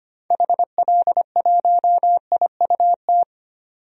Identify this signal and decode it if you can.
Morse code, heard as a keyed tone, about 25 wpm, 700 Hz. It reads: HL1IUT
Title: DXCC-100 Call Signs - 6 Characters - 25wpm